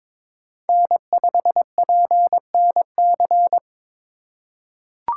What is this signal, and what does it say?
Morse code, heard as a keyed tone, about 21 wpm, 700 Hz. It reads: N5PNC
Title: USA Call Signs - 5 Characters - 2x Character Spacing - 22wpm